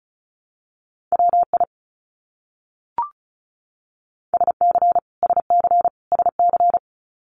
Morse code, heard as a keyed tone, about 34 wpm, 700 Hz. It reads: WI SC SC SC
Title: US States Repeated 3x 35wpm